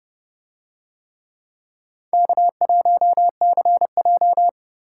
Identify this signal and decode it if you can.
Morse code, heard as a keyed tone, about 29 wpm, 700 Hz. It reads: K1CJ